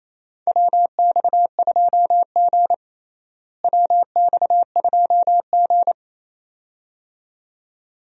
Morse code, heard as a keyed tone, about 28 wpm, 700 Hz. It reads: WX2G WX2G
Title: USA Call Signs - 4 Characters - Repeated 2x - 28wpm